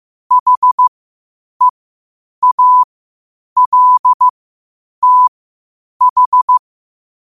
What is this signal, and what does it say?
Morse code, heard as a keyed tone, about 15 wpm, spaced as for 8 wpm, 1000 Hz. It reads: HEALTH
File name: qotdfiles / 8wpm